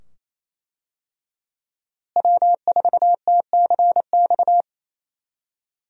Morse code, heard as a keyed tone, about 28 wpm, 700 Hz. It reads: W4TCX
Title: USA Call Signs - 5 Characters - Repeated 2x - 28wpm